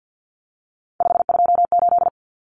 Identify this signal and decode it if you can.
Morse code, heard as a keyed tone, about 50 wpm, 700 Hz. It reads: 528
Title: Common Cut Numbers - 3-Digit 50wpm